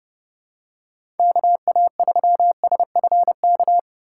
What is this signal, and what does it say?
Morse code, heard as a keyed tone, about 30 wpm, 700 Hz. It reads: KA3SFK
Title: USA Call Signs - 6 Characters - 4x Character Spacing - 30wpm